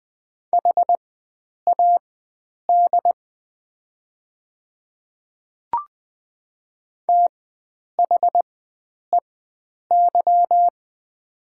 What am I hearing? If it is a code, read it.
Morse code, heard as a keyed tone, about 20 wpm, spaced as for 9 wpm, 700 Hz. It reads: HAD THEY
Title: 4x Character Spacing - Top 100 Words - 20wpm